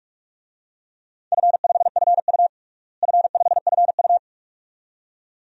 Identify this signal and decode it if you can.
Morse code, heard as a keyed tone, about 47 wpm, 700 Hz. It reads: F5FU F5FU
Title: DXCC-100 Call Signs - 4 Characters - Repeated 2x - 45wpm